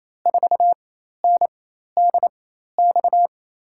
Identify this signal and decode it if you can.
Morse code, heard as a keyed tone, about 28 wpm, spaced as for 13 wpm, 700 Hz. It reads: 4NDX
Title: USA Call Signs - 6 Characters - 4x Character Spacing - 28wpm